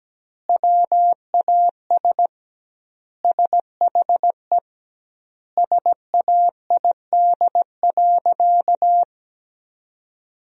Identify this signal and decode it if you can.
Morse code, heard as a keyed tone, about 17 wpm, 700 Hz. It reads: WAS SHE SAID.
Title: 2x Word-Spacing - Sentences from Top 400 Words - 17wpm